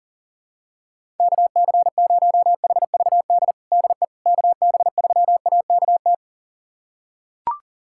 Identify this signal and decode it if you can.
Morse code, heard as a keyed tone, about 40 wpm, 700 Hz. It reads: KC0HVD DE KB3AKT
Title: USA Callsign DE Callsign 40wpm